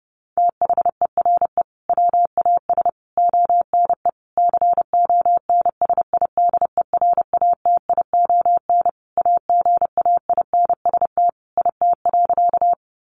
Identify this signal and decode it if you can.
Morse code, heard as a keyed tone, about 30 wpm, 700 Hz. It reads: THERE WAS ONE CONSIDERATION AGAINST IT.
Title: Sentences from Top 1500 Words 30wpm